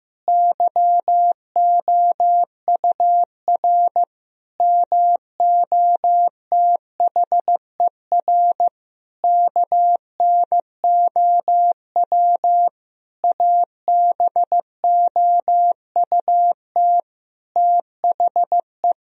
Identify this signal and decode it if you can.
Morse code, heard as a keyed tone, about 15 wpm, 700 Hz. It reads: YOUR MOTHER KNOW ABOUT THE
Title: Sentences from Top 1000 Words - Review Part 4 - 15wpm